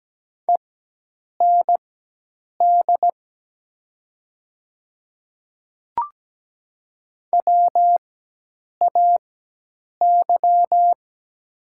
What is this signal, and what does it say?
Morse code, heard as a keyed tone, about 17 wpm, spaced as for 8 wpm, 700 Hz. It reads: END WAY
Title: 4x Character Spacing - Top 200 Words - 17wpm